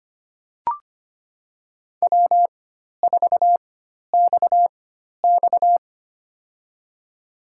Morse code, heard as a keyed tone, about 25 wpm, spaced as for 12 wpm, 700 Hz. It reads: W4XX
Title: USA Call Signs - 4 Characters - 4x Character Spacing - 25wpm